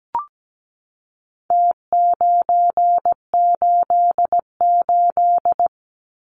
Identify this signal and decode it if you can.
Morse code, heard as a keyed tone, about 17 wpm, 700 Hz. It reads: T988
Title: Numbers - Contest Style - 4-Digit 17wpm